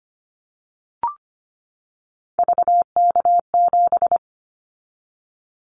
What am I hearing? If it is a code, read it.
Morse code, heard as a keyed tone, about 25 wpm, 700 Hz. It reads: VK7
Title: Callsign Prefixes for Top 100 DXCC - 25wpm